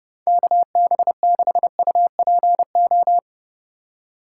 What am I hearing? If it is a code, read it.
Morse code, heard as a keyed tone, about 30 wpm, 700 Hz. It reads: KB6UPO
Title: USA Call Signs - 6 Characters - 2x Character Spacing - 30wpm